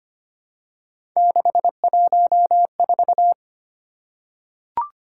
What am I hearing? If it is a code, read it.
Morse code, heard as a keyed tone, about 25 wpm, 700 Hz. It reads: 614
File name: Three Letter-Number Combinations - 25wpm